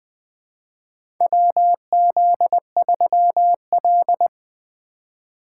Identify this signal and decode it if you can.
Morse code, heard as a keyed tone, about 20 wpm, 700 Hz. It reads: WZ3L
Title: USA Call Signs - 4 Characters - 3x Character Spacing - 20wpm